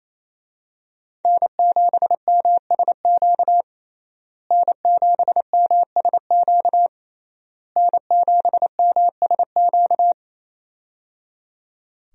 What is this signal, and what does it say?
Morse code, heard as a keyed tone, about 28 wpm, 700 Hz. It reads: N7MSQ N7MSQ N7MSQ
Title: USA Call Signs - 5 Characters - Repeated 3x - 28wpm